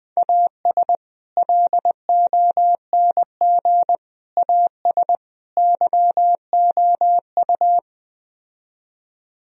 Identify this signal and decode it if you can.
Morse code, heard as a keyed tone, about 20 wpm, 700 Hz. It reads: AS LONG AS YOU
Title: Speed-Racing - Sets of 4 Words 17wpm